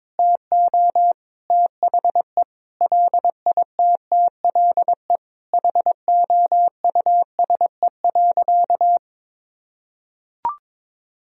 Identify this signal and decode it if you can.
Morse code, heard as a keyed tone, about 22 wpm, 700 Hz. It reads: TO THE LITTLE HOUSE.
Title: Sentences from Top 200 Words and Abbreviations 22wpm